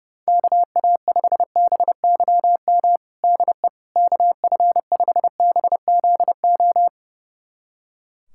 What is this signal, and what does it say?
Morse code, heard as a keyed tone, about 30 wpm, 700 Hz. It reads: KA5BYM DE KF5BZO
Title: USA Callsign DE Callsign 30wpm